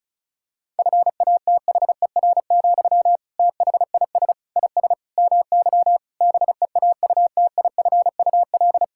Morse code, heard as a keyed tone, about 34 wpm, 700 Hz. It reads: FATHER, THIS IS MY BEAUTIFUL
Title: Sentences from Top 1000 Words - Review Part 6 - 35wpm